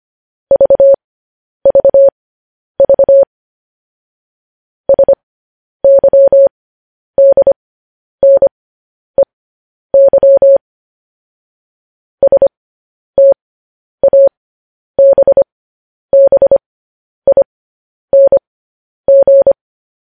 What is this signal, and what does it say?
Morse code, heard as a keyed tone, about 25 wpm, spaced as for 10 wpm, 550 Hz. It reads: VVV SYDNEY STABBING
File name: podcast / News headlines-1x-25x10